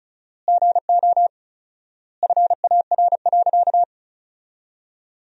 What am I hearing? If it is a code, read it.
Morse code, heard as a keyed tone, about 34 wpm, 700 Hz. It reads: GO FAR.
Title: 4x Word-Spacing - Sentences from Top 400 Words - 35wpm